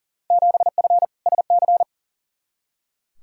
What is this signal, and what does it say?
Morse code, heard as a keyed tone, about 40 wpm, 700 Hz. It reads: 7F SC
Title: Field Day Exchanges Encore - 40wpm